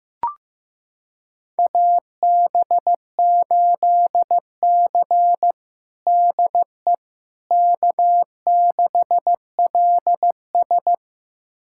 Morse code, heard as a keyed tone, about 16 wpm, 700 Hz. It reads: AB8C DE K6LS
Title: USA Callsign DE Callsign - 4 Characters - 15wpm